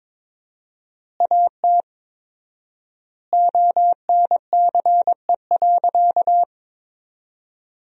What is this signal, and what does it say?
Morse code, heard as a keyed tone, about 22 wpm, 700 Hz. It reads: AT ONCE.
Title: 4x Word-Spacing - Sentences from Top 300 Words - 22wpm